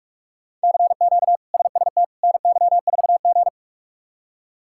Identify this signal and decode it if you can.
Morse code, heard as a keyed tone, about 41 wpm, 700 Hz. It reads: CQ SST NY4G
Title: K1USN SST QSO Practice - 45wpm